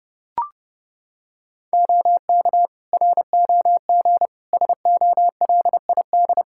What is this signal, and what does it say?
Morse code, heard as a keyed tone, about 31 wpm, 700 Hz. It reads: OK ROG SOLID
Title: Practice QSO - Exchange 3 - More Info - 30wpm